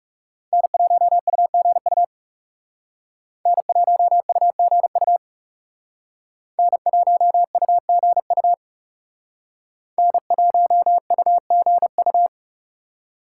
Morse code, heard as a keyed tone, about 38 wpm, 700 Hz. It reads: N1UGU N1UGU N1UGU N1UGU